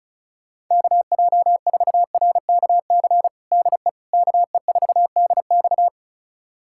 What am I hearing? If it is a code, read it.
Morse code, heard as a keyed tone, about 36 wpm, 700 Hz. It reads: KJ4RKC DE KE4DX
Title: USA Callsign DE Callsign 35wpm